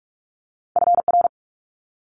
Morse code, heard as a keyed tone, about 46 wpm, 700 Hz. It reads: FR
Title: Multi-letter Phonemes Quick Fire 45wpm